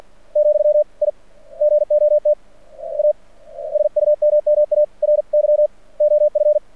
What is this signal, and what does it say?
Morse code, heard as a keyed tone, about 48 wpm, 600 Hz. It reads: , I GOT A SUMMARY OF
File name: archived / CRSnet